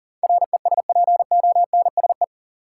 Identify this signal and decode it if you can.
Morse code, heard as a keyed tone, about 40 wpm, 700 Hz. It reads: RESPONSE
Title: Sentences from Top 1400 Words - New Word Drill Alt - 40wpm